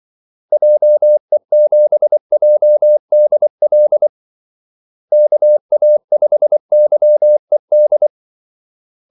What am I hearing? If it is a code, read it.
Morse code, heard as a keyed tone, about 24 wpm, 600 Hz. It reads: JE7JDL KA5YED
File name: Callsigns-24wpm